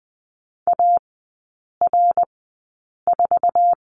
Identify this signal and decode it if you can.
Morse code, heard as a keyed tone, about 20 wpm, spaced as for 8 wpm, 690 Hz. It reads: AR4
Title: 060 Lesson 11e - Letter h - Three Characters 20wpm